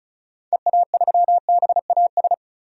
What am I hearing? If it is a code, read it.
Morse code, heard as a keyed tone, about 36 wpm, 700 Hz. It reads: EA3BAS
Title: DXCC-100 Call Signs - 6 Characters - 35wpm